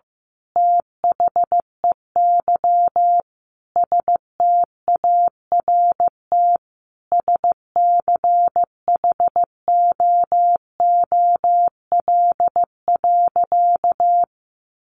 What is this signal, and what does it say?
Morse code, heard as a keyed tone, about 15 wpm, 700 Hz. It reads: THEY START SCHOOL.
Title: Sentences from Top 1100 Words 15wpm